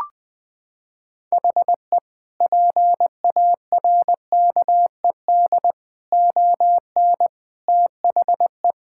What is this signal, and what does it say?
Morse code, heard as a keyed tone, about 20 wpm, 700 Hz. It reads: HE PARKED ON THE
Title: Sentences with 4-Word Phrases - 20wpm